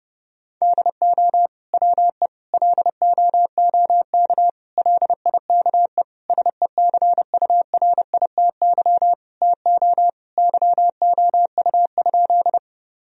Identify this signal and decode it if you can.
Morse code, heard as a keyed tone, about 30 wpm, 700 Hz. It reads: DO WE LOOK LIKE SECURITY TO YOU?
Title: Sentences from Top 1000 Words - Review Part 4 - 30wpm